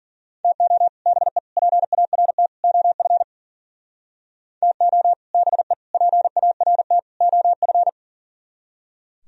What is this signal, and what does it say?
Morse code, heard as a keyed tone, about 51 wpm, 700 Hz. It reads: TO BE PART OF TO BE PART OF